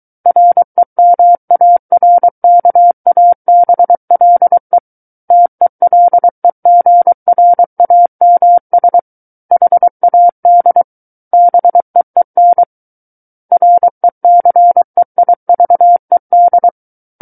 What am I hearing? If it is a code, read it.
Morse code, heard as a keyed tone, about 23 wpm, 700 Hz. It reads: REMARKABLE TELEGRAMS HAD BEEN RECEIVED
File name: Book One / 14-Chapter 14 - 23 WPM